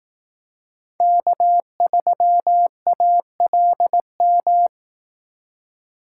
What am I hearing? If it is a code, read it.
Morse code, heard as a keyed tone, about 18 wpm, 700 Hz. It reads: K3ALM
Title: Speed Racing - USA 1x3 Call Signs 15wpm